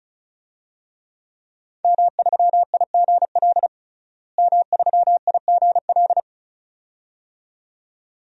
Morse code, heard as a keyed tone, about 36 wpm, 700 Hz. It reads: M3IGL M3IGL
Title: DXCC-100 Call Signs - 5 Characters - Repeated 2x - 35wpm